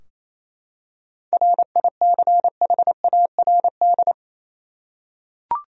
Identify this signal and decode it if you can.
Morse code, heard as a keyed tone, about 28 wpm, 700 Hz. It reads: RICHARD
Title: USA Common Names - 28wpm